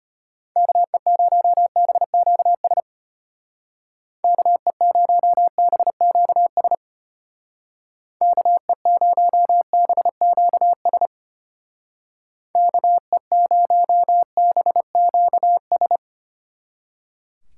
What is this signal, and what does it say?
Morse code, heard as a keyed tone, about 31 wpm, 700 Hz. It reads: KE0BQS KE0BQS KE0BQS KE0BQS
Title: Speed Racing - USA 2x3 Call Signs 25wpm